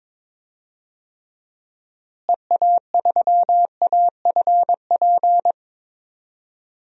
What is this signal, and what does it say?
Morse code, heard as a keyed tone, about 22 wpm, 700 Hz. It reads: EA3AFP
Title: DXCC-100 Call Signs - 6 Characters - Repeated 2x - 22wpm